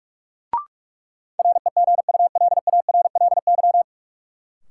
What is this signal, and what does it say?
Morse code, heard as a keyed tone, about 47 wpm, 700 Hz. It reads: REGULARLY
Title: Sentences from Top 1400 Words - New Word Drill Alt - 45wpm